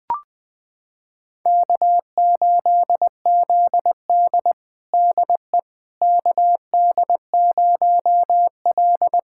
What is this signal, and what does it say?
Morse code, heard as a keyed tone, about 20 wpm, 700 Hz. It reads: K8ZD DE KD0L
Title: USA Callsign DE Callsign - 4 Characters - 20wpm